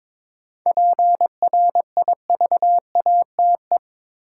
Morse code, heard as a keyed tone, about 21 wpm, 700 Hz. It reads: PRIVATE
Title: Sentences from Top 1000 Words - New Word Drill - 22wpm